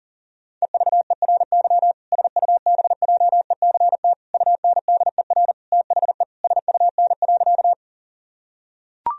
Morse code, heard as a keyed tone, about 37 wpm, 700 Hz. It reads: EVERY SUBJECT UNDER THE SUN.
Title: Sentences from Top 1000 Words - Review Part 1 - 40wpm